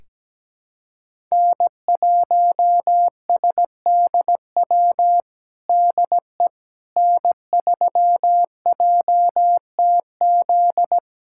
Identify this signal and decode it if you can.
Morse code, heard as a keyed tone, about 17 wpm, 700 Hz. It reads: N1SDW DE N3JTZ
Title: USA Callsign DE Callsign - 5 Characters - 17wpm